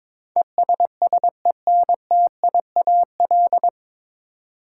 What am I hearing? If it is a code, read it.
Morse code, heard as a keyed tone, about 23 wpm, 700 Hz. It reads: ESSENTIAL
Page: Morse Code Ninja practice 22wpm